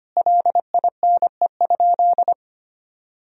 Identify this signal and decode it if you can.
Morse code, heard as a keyed tone, about 25 wpm, 700 Hz. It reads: LINE?
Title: Sentences from Top 1000 Words - Review Part 1 - 25wpm